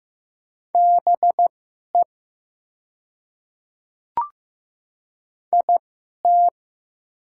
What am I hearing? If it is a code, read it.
Morse code, heard as a keyed tone, about 15 wpm, spaced as for 11 wpm, 690 Hz. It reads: BE IT
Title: 2x Character Spacing - Top 300 Words - 15wpm